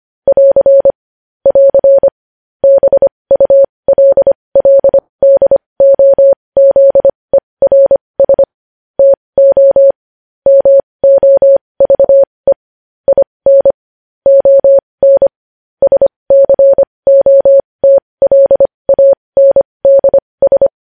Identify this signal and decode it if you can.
Morse code, heard as a keyed tone, about 25 wpm, spaced as for 20 wpm, 550 Hz. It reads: <AR> <AR> BULLDOZERS TO MOVE IN ON SCOTLANDS
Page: news headlines in morse code at 20 wpm